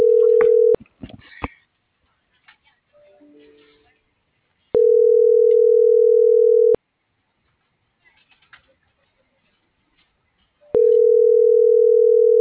Ringback tone (North America), 440 + 480 Hz.